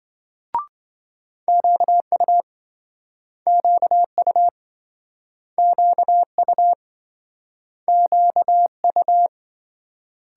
Morse code, heard as a keyed tone, about 26 wpm, 700 Hz. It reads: QU QU QU QU